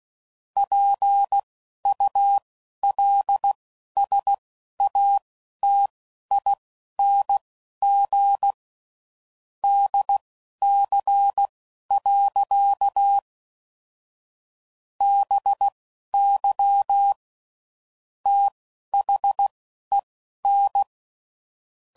Morse code, heard as a keyed tone, about 16 wpm, spaced as for 11 wpm, 800 Hz. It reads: PULSATING DC. BY THEN
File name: CW Code Groups @10WPM / Technical Subjects / SwPwrSup